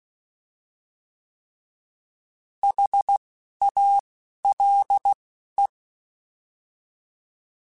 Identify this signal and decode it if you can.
Morse code, heard as a keyed tone, about 16 wpm, 800 Hz. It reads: H A L E